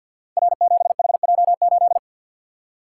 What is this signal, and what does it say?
Morse code, heard as a keyed tone, about 51 wpm, 700 Hz. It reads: R7HJ8
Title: Code Groups - 5 Letter:Numbers - 50wpm